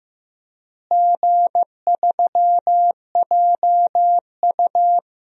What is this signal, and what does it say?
Morse code, heard as a keyed tone, about 15 wpm, 700 Hz. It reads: G3JU
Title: DXCC-100 Call Signs - 4 Characters - Repeated 3x - 15wpm